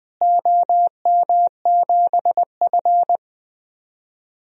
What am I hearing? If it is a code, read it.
Morse code, heard as a keyed tone, about 20 wpm, 700 Hz. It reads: OM7F
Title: DXCC-100 Call Signs - 4 Characters - Repeated 2x - 20wpm